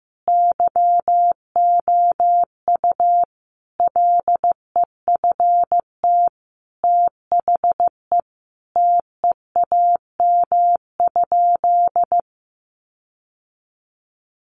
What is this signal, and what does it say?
Morse code, heard as a keyed tone, about 15 wpm, 700 Hz. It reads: YOU LEFT THE TEAM?
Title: Sentences from Top 1500 Words 15wpm